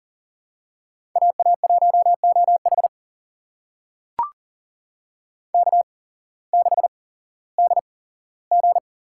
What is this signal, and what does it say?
Morse code, heard as a keyed tone, about 42 wpm, spaced as for 19 wpm, 700 Hz. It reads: AA1OH K6DG
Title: USA Call Signs - 8x Character Spacing - 40wpm